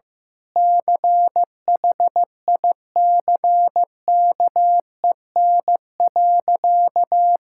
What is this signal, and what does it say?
Morse code, heard as a keyed tone, about 15 wpm, 700 Hz. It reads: CHICKEN.
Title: Sentences from Top 1000 Words - Review Part 5 - 15wpm